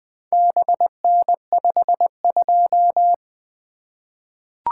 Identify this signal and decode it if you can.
Morse code, heard as a keyed tone, about 21 wpm, 700 Hz. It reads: BN52
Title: Four Letter-Number Combinations - 20wpm